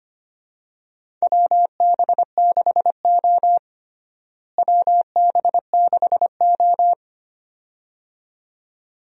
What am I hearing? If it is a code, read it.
Morse code, heard as a keyed tone, about 25 wpm, 700 Hz. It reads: WB6O WB6O
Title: USA Call Signs - 4 Characters - Repeated 2x - 25wpm